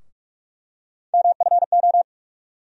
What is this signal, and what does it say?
Morse code, heard as a keyed tone, about 46 wpm, 700 Hz. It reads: MFO
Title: USA Call Sign Postfixes - 3 Letters 45wpm